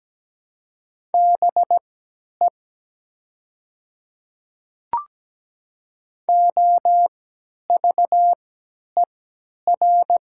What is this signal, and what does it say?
Morse code, heard as a keyed tone, about 17 wpm, spaced as for 10 wpm, 700 Hz. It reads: BE OVER
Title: 3x Character Spacing - Top 100 Words - 17wpm